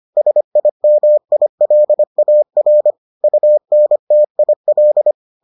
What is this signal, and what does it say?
Morse code, heard as a keyed tone, about 25 wpm, 600 Hz. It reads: SIMILAR UNTIL